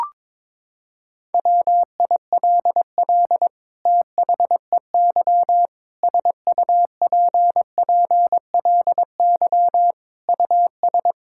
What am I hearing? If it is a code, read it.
Morse code, heard as a keyed tone, about 22 wpm, 700 Hz. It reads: WILL THEY SUPPLY US
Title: Sentences from Top 1000 Words - Review Part 5 - 22wpm